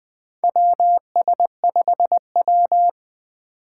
Morse code, heard as a keyed tone, about 20 wpm, 700 Hz. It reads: WS5W
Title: Four Challenging Letter-Number Combinations 20wpm